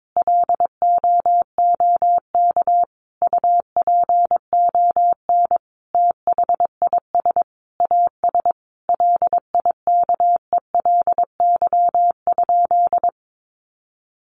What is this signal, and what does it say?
Morse code, heard as a keyed tone, about 22 wpm, 700 Hz. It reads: LOOK UPON THIS AS LIKELY?
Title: Sentences from Top 1100 Words 22wpm